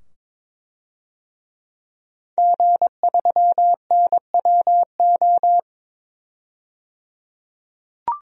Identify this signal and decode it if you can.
Morse code, heard as a keyed tone, about 22 wpm, 700 Hz. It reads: G3NWO
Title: DXCC-100 Call Signs - 5 Characters - 22wpm